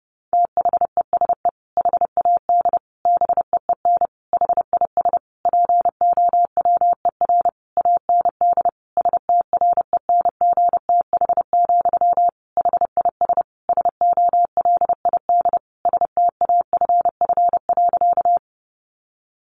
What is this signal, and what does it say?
Morse code, heard as a keyed tone, about 30 wpm, 700 Hz. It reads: THESE HAD BEEN HIS POWER AND STRENGTH, HIS SOLID STAFF.
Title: Sentences from Top 1500 Words 30wpm